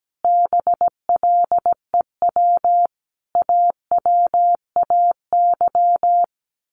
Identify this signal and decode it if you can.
Morse code, heard as a keyed tone, about 17 wpm, 700 Hz. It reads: BLEW AWAY
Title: A Sound Of Thunder 17wpm